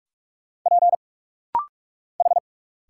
Morse code, heard as a keyed tone, about 46 wpm, 700 Hz. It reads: P H
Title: Single Letter-Number - Rapid-Fire 45wpm